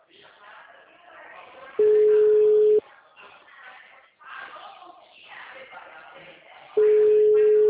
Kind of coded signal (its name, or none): telephone call progress tone